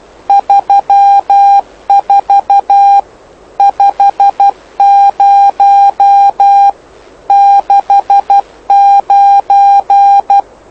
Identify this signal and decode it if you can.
Morse code, heard as a keyed tone, about 12 wpm, 800 Hz. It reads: 34 50 69
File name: lu4aao / lu7aa 09dic2011 exp planeador vuelo1